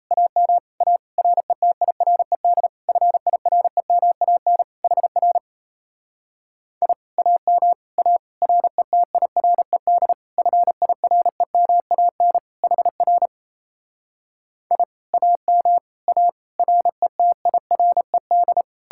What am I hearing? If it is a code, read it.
Morse code, heard as a keyed tone, about 37 wpm, 700 Hz. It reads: AM A RETIRED FIREMAN HR I AM A RETIRED FIREMAN HR I AM A RETIRED